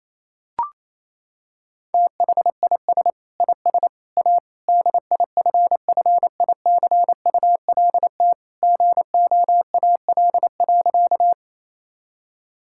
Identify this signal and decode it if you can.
Morse code, heard as a keyed tone, about 28 wpm, 700 Hz. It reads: THIS IS A DIFFICULT GOAL.
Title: Sentences from Top 1000 Words - Review Part 5 - 28wpm